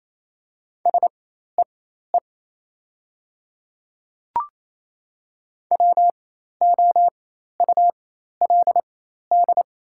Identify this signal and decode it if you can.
Morse code, heard as a keyed tone, about 27 wpm, spaced as for 13 wpm, 700 Hz. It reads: SEE WOULD
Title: 4x Character Spacing - Top 200 Words - 28wpm